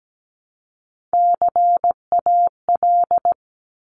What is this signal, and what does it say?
Morse code, heard as a keyed tone, about 17 wpm, 700 Hz. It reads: CAL
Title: Top Three Letter Combinations in Words 17wpm